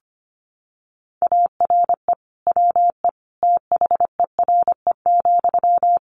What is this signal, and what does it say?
Morse code, heard as a keyed tone, about 25 wpm, 700 Hz. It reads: ARE WE THERE,
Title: Sentences from Top 1300 Words 25wpm